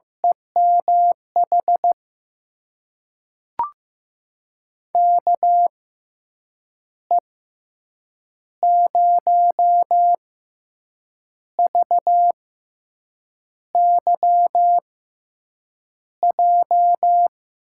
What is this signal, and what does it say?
Morse code, heard as a keyed tone, about 15 wpm, spaced as for 6 wpm, 700 Hz. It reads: EMH KE0VYJ
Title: USA Call Signs - 6 Characters - 6x Character Spacing - 15wpm